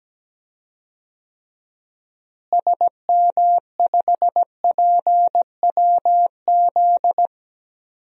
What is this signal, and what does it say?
Morse code, heard as a keyed tone, about 17 wpm, 700 Hz. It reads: SM5PWZ
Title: DXCC-100 Call Signs - 6 Characters - Repeated 3x - 17wpm